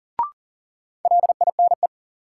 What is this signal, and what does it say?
Morse code, heard as a keyed tone, about 40 wpm, 700 Hz. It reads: LINE
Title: Sentences from Top 1200 Words - New Word Drill Alt - 40wpm